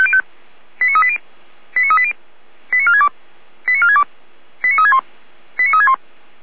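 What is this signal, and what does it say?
Five-tone selective call (ZVEI), heard as quick runs of five tones, tones of about 70 ms. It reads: 89380 89380 89462 89462 89361 89361
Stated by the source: ZVEI five-tone call received by SDR, server of Feuerwehr Orlamünde fire brigade.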